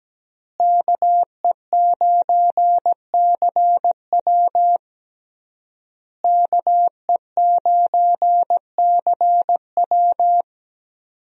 Morse code, heard as a keyed tone, about 17 wpm, 700 Hz. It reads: KE9CW KE9CW
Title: USA Call Signs - 5 Characters - Repeated 3x - 17wpm